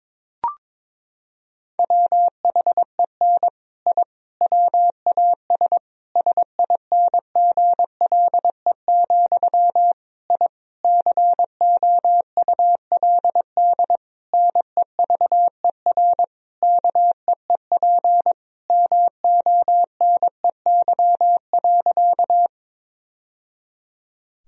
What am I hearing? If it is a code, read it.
Morse code, heard as a keyed tone, about 22 wpm, 700 Hz. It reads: WHEN I WAS SINGLE, I COULD NEVER KEEP MONEY.
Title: Sentences from Top 1000 Words - Review Part 4 - 22wpm